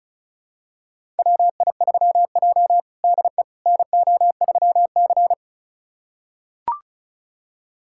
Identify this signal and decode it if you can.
Morse code, heard as a keyed tone, about 36 wpm, 700 Hz. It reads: WI3J DE NO3C
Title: USA Callsign DE Callsign - 4 Characters - 35wpm